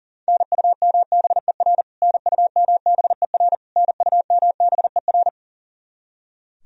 Morse code, heard as a keyed tone, about 37 wpm, 700 Hz. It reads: NUMBER NUMBER NUMBER
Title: Top 100 Words Repeated 40wpm